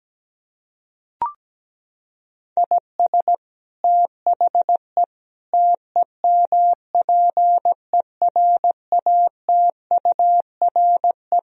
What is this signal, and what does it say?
Morse code, heard as a keyed tone, about 17 wpm, 700 Hz. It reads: IS THE TEMPERATURE
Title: Sentences from Top 1000 Words - Review Part 4 - 17wpm